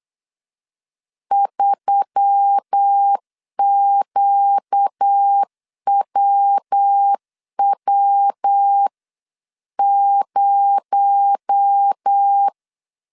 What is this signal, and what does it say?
Morse code, heard as a keyed tone, about 8 wpm, 800 Hz. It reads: 3QWW0
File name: CW / DET / 1 6 1500C S B 8wpm 5142